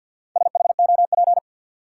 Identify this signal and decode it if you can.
Morse code, heard as a keyed tone, about 49 wpm, 700 Hz. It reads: SHOP
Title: Sentences from Top 1600 Words - New Word Drill - 50wpm